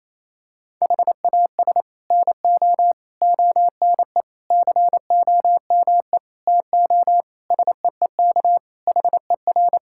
Morse code, heard as a keyed tone, about 28 wpm, 700 Hz. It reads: HAS NO ONE COME TO SEEK HER
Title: Sentences from Top 1600 Words 28wpm